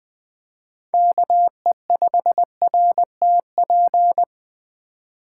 Morse code, heard as a keyed tone, about 20 wpm, 700 Hz. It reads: KE5RTP
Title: Speed Racing - USA 2x3 Call Signs 17wpm